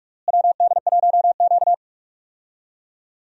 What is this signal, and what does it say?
Morse code, heard as a keyed tone, about 44 wpm, 700 Hz. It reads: WD1Q
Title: USA Call Signs - 4 Characters - Repeated 2x - 45wpm